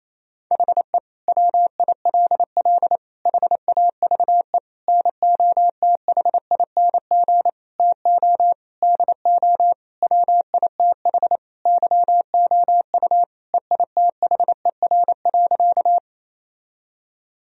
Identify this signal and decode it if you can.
Morse code, heard as a keyed tone, about 28 wpm, 700 Hz. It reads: HE WILL HAVE NOTHING TO DO WITH YOU EITHER.
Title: Sentences from Top 1000 Words - Review Part 4 - 28wpm